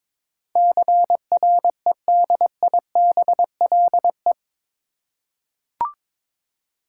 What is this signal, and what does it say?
Morse code, heard as a keyed tone, about 22 wpm, 700 Hz. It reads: CREDIBLE
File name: practice / 22wpm / Eight Letter Words - 22wpm